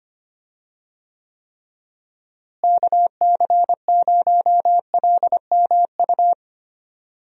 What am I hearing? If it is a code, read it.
Morse code, heard as a keyed tone, about 25 wpm, 700 Hz. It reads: KC0LMU
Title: USA Call Signs - 6 Characters - Repeated 3x - 25wpm